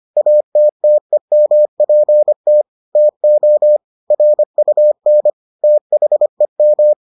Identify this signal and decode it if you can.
Morse code, heard as a keyed tone, about 25 wpm, 600 Hz. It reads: ATTEMPT TO RUN THEM